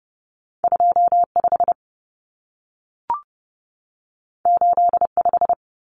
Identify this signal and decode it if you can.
Morse code, heard as a keyed tone, about 30 wpm, 700 Hz. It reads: 25 85
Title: Numbers - 2-Digit 30wpm